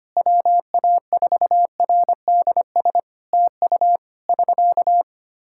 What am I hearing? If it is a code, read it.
Morse code, heard as a keyed tone, about 25 wpm, 700 Hz. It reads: WA4RDS TU <SK>
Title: Practice QSO - Exchange 4 - Ending - 25wpm